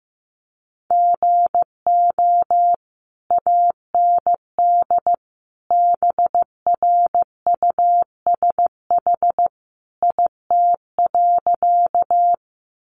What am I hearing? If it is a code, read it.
Morse code, heard as a keyed tone, about 15 wpm, 700 Hz. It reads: GO AND BRUSH IT.
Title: Sentences from Top 1100 Words 15wpm